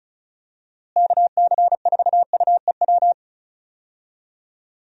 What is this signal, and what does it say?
Morse code, heard as a keyed tone, about 36 wpm, 700 Hz. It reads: KC4UEW
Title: USA Call Signs - 6 Characters - Repeated 2x - 35wpm